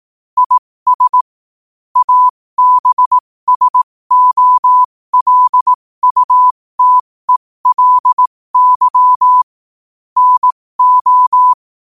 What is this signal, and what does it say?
Morse code, heard as a keyed tone, about 18 wpm, spaced as for 15 wpm, 1000 Hz. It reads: IS ABSOLUTELY NO
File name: qotdfiles / 15wpm